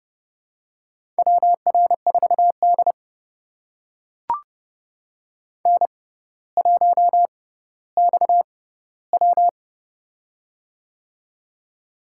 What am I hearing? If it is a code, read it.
Morse code, heard as a keyed tone, about 30 wpm, spaced as for 15 wpm, 700 Hz. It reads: WR4D N1XW
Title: USA Call Signs - 4 Characters - 6x Character Spacing - 30wpm